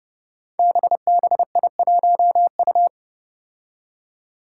Morse code, heard as a keyed tone, about 30 wpm, 700 Hz. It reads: BBI1U
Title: Five Challenging Letter-Number Combinations 30wpm